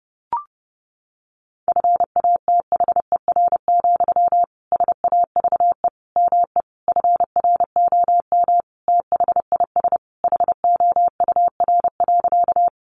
Morse code, heard as a keyed tone, about 30 wpm, 700 Hz. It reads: FATHER, SAVE ME FROM THIS HOUR.
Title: Sentences from Top 1100 Words 30wpm